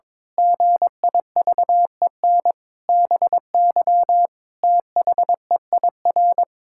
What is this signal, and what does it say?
Morse code, heard as a keyed tone, about 22 wpm, 700 Hz. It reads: GIVEN BY THEIR
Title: Pangram Challenge v2 - 22wpm